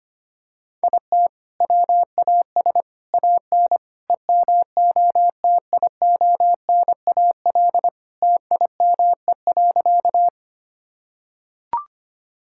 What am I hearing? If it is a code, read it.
Morse code, heard as a keyed tone, about 25 wpm, 700 Hz. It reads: IT WAS AN EMOTIONAL TIME.
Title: Sentences from Top 1000 Words - Review Part 5 - 25wpm